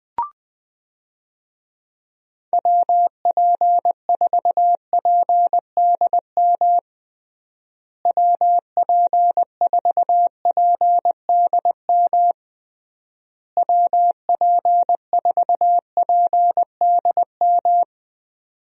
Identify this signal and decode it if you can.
Morse code, heard as a keyed tone, about 20 wpm, 700 Hz. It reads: WP4PDM WP4PDM WP4PDM
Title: USA Call Signs - 6 Characters - Repeated 3x - 20wpm